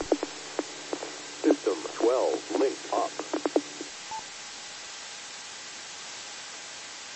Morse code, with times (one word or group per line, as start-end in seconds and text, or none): none